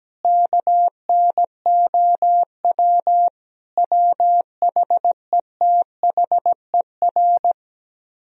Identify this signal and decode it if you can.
Morse code, heard as a keyed tone, about 17 wpm, 700 Hz. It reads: KNOW WHETHER
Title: Speed-Racing - Sets of 5 Words 17wpm